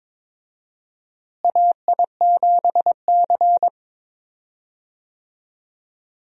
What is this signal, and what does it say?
Morse code, heard as a keyed tone, about 22 wpm, 700 Hz. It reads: AI7C
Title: USA Call Signs - 4 Characters - Repeated 2x - 22wpm